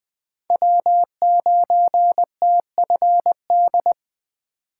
Morse code, heard as a keyed tone, about 22 wpm, 700 Hz. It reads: W9TFD